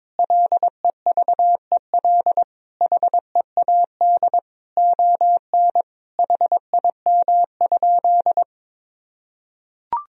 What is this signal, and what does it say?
Morse code, heard as a keyed tone, about 22 wpm, 700 Hz. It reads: LEVEL HEAD ON HIM?
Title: Sentences from Top 1000 Words - Review Part 4 - 22wpm